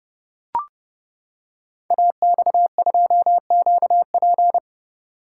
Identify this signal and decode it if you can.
Morse code, heard as a keyed tone, about 29 wpm, 700 Hz. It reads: AX2QP
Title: Code Groups - 5 Letter:Numbers - 30wpm